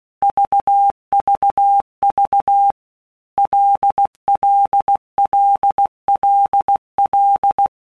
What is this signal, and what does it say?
Morse code, heard as a keyed tone, about 16 wpm, 800 Hz. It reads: VVV LLLLL
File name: Basic Letter Groups By Blocks / Groups@15 / LFCYQ@15